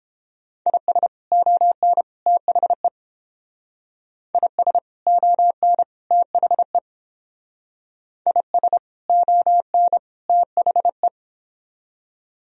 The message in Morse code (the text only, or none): IS ON THE IS ON THE IS ON THE